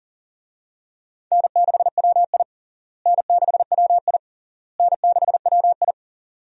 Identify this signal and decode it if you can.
Morse code, heard as a keyed tone, about 40 wpm, 700 Hz. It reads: N6WI N6WI N6WI
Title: USA Call Signs - 4 Characters - Repeated 3x - 40wpm